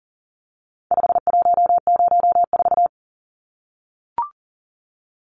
Morse code, heard as a keyed tone, about 40 wpm, 700 Hz. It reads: 5104
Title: Numbers - 4-Digit 40wpm